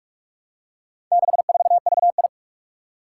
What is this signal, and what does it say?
Morse code, heard as a keyed tone, about 44 wpm, 700 Hz. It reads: 64VI
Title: Four Challenging Letter-Number Combinations 45wpm